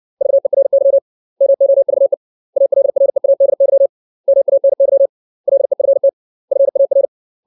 Morse code, heard as a keyed tone, about 60 wpm, 550 Hz. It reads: VERY MOVE ALREADY MANY LEFT FAR
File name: 632 words / 632 Words 60WPM